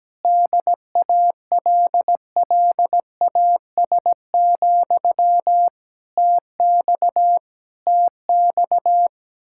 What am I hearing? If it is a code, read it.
Morse code, heard as a keyed tone, about 17 wpm, 700 Hz. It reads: DALLAS, TX TX
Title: Speed-Racing Common QSO Phrases - 17wpm